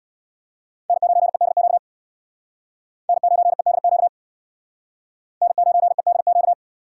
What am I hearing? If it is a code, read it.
Morse code, heard as a keyed tone, about 62 wpm, 700 Hz. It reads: N9EDQ N9EDQ N9EDQ
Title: Speed Racing - USA 1x3 Call Signs 50wpm